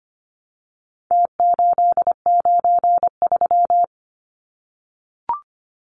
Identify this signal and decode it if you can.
Morse code, heard as a keyed tone, about 25 wpm, 700 Hz. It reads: T893
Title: Numbers - Contest Style - 4-Digit 25wpm